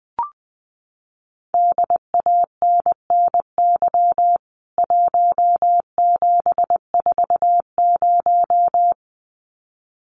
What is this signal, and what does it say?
Morse code, heard as a keyed tone, about 20 wpm, 700 Hz. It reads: DANNY 1740
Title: CWT Exchange 20wpm